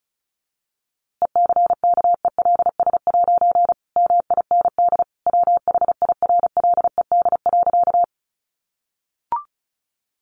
Morse code, heard as a keyed tone, about 34 wpm, 700 Hz. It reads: ECKELS' MIND WHIRLED.
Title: A Sound Of Thunder 35wpm